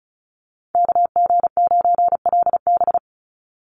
Morse code, heard as a keyed tone, about 34 wpm, 700 Hz. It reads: KG9LB
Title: Call Signs - 35wpm